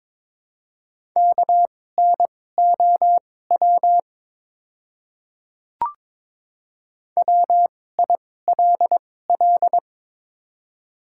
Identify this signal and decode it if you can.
Morse code, heard as a keyed tone, about 23 wpm, spaced as for 16 wpm, 700 Hz. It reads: KNOW WILL
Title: 2x Character Spacing - Top 200 Words - 22wpm